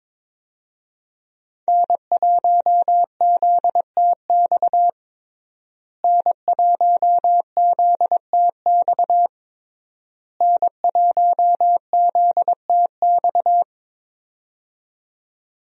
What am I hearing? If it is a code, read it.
Morse code, heard as a keyed tone, about 22 wpm, 690 Hz. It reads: N1ZTX N1ZTX N1ZTX
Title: USA Call Signs - 5 Characters - Repeated 3x - 22wpm